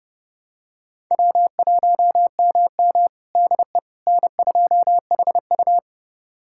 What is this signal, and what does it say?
Morse code, heard as a keyed tone, about 31 wpm, 700 Hz. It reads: W1MM DE N2HU
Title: USA Callsign DE Callsign - 4 Characters - 30wpm